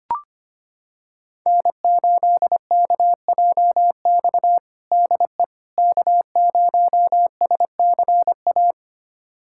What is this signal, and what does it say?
Morse code, heard as a keyed tone, about 25 wpm, 700 Hz. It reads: N8KJX DE K0SCA
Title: USA Callsign DE Callsign - 5 Characters - 25wpm